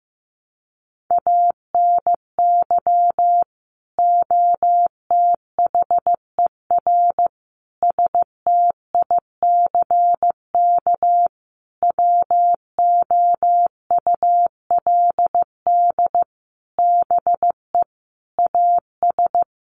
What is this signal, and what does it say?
Morse code, heard as a keyed tone, about 15 wpm, 700 Hz. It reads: ANY OTHER STICK WOULD BE AS
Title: Sentences from Top 1200 Words 15wpm